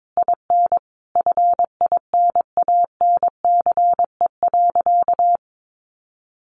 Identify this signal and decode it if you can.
Morse code, heard as a keyed tone, about 22 wpm, 700 Hz. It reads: IN FINANCE.
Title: Sentences from Top 1500 Words 22wpm